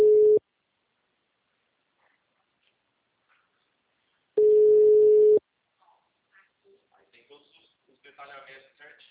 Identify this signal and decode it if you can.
Ringback tone (Europe, 425 Hz), with speech.
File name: out- -2001- - - .2083